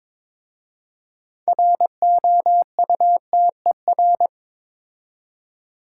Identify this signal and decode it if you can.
Morse code, heard as a keyed tone, about 21 wpm, 700 Hz. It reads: ROUTER
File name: practice / 22wpm / Difficult Words with Hints - 22wpm